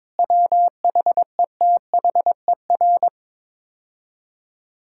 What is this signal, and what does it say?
Morse code, heard as a keyed tone, about 22 wpm, 700 Hz. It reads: WHETHER
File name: practice / 22wpm / Sets of 6 Words - 22wpm